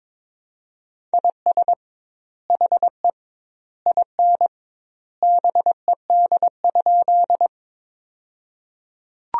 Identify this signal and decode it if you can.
Morse code, heard as a keyed tone, about 22 wpm, 700 Hz. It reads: IS HE IN BED?
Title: 2x Word-Spacing - Sentences from Top 500 Words - 22wpm